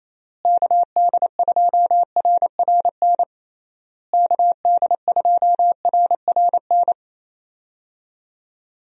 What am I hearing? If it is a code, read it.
Morse code, heard as a keyed tone, about 28 wpm, 700 Hz. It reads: KD2RRN KD2RRN
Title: USA Call Signs - 6 Characters - Repeated 3x - 28wpm